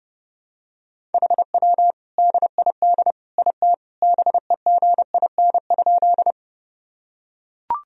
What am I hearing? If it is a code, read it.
Morse code, heard as a keyed tone, about 28 wpm, 700 Hz. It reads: HW DID IT BEGIN?
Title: Sentences from Top 300 Words and Abbreviations 30wpm